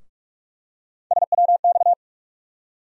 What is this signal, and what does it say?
Morse code, heard as a keyed tone, about 46 wpm, 700 Hz. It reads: SWX
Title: USA Call Sign Postfixes - 3 Letters 45wpm